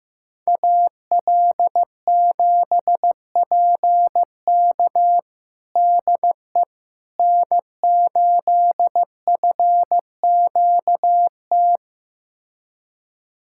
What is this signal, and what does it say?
Morse code, heard as a keyed tone, about 15 wpm, 700 Hz. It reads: AL7PK DE N8FQT
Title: USA Callsign DE Callsign - 5 Characters - 15wpm